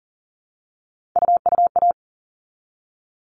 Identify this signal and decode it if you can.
Morse code, heard as a keyed tone, about 40 wpm, 700 Hz. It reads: UUA
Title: All Cut Numbers - 3-Digit 40wpm